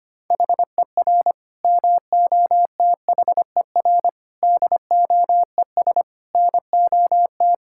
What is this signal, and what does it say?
Morse code, heard as a keyed tone, about 25 wpm, 700 Hz. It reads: HER MOTHER DOES NOT
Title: Sentences from Top 1000 Words - Review Part 5 - 25wpm